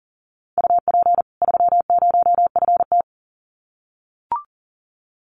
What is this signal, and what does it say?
Morse code, heard as a keyed tone, about 40 wpm, 700 Hz. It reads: UP 30FT
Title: Common QSO Phrases 40wpm